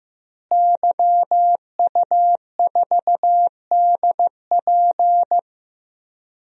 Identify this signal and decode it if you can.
Morse code, heard as a keyed tone, about 15 wpm, 700 Hz. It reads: YU4DP
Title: DXCC-100 Call Signs - 5 Characters - Repeated 2x - 15wpm